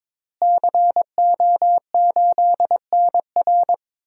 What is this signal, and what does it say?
Morse code, heard as a keyed tone, about 21 wpm, 700 Hz. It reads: CO8NR
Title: DXCC-100 Call Signs - 5 Characters - Repeated 3x - 22wpm